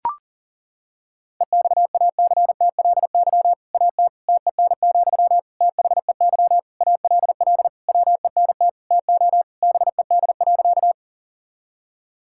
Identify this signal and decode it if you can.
Morse code, heard as a keyed tone, about 37 wpm, 700 Hz. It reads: EXACTLY AT TEN, THEY ALL WENT TO BED.
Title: Sentences from Top 1000 Words - Review Part 6 - 40wpm